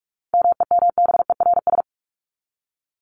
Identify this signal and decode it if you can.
Morse code, heard as a keyed tone, about 44 wpm, 700 Hz. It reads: MEMBERS
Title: ES:S Suffix - Top 100 Words with Base Words 45wpm